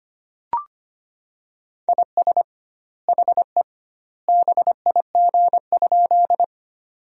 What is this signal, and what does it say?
Morse code, heard as a keyed tone, about 25 wpm, 700 Hz. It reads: IS HE BIG?
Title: 2x Word-Spacing - Sentences from Top 200 Words - 25wpm